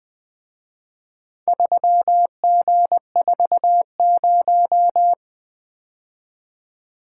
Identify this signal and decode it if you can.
Morse code, heard as a keyed tone, about 24 wpm, 700 Hz. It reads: 3G40
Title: Speed-Racing - Four Letter-Number Combinations - 20wpm